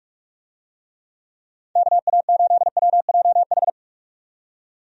Morse code, heard as a keyed tone, about 46 wpm, 700 Hz. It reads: KA8WJH
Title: USA Call Signs - 6 Characters - 45wpm